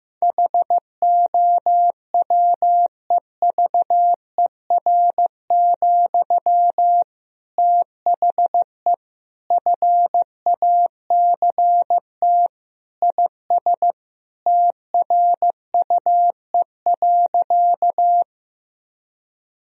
Morse code, heard as a keyed tone, about 15 wpm, 700 Hz. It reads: HOWEVER, THE FACT IS TRUE.
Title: Sentences from Top 1000 Words - Review Part 3 - 15wpm